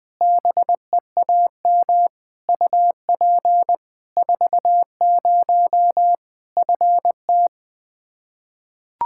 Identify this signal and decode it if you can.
Morse code, heard as a keyed tone, about 20 wpm, 700 Hz. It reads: BEAM UP 40 FT
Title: Practice QSO - Exchange 2 - Info - 20wpm